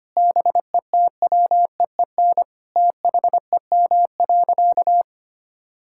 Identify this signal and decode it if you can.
Morse code, heard as a keyed tone, about 25 wpm, 700 Hz. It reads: BETWEEN THEM.
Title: Sentences from Top 1000 Words - Review Part 6 - 25wpm